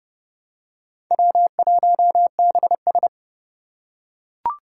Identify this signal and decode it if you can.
Morse code, heard as a keyed tone, about 30 wpm, 700 Hz. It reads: W1BS
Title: Four Challenging Letter-Number Combinations 30wpm